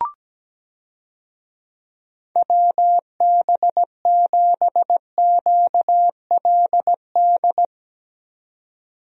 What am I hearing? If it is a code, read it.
Morse code, heard as a keyed tone, about 17 wpm, 700 Hz. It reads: WB7QLD
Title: USA Call Signs - 6 Characters - 17wpm